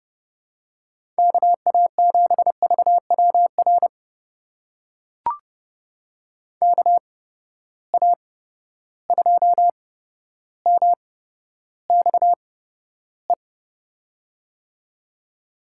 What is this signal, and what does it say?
Morse code, heard as a keyed tone, about 30 wpm, spaced as for 13 wpm, 700 Hz. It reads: KA7VWR KA2MXE
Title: USA Call Signs - 6 Characters - 8x Character Spacing - 30wpm